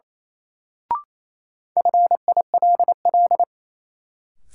Morse code, heard as a keyed tone, about 28 wpm, 700 Hz. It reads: FILL
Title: Sentences from Top 1500 Words - New Word Drill Alt - 28wpm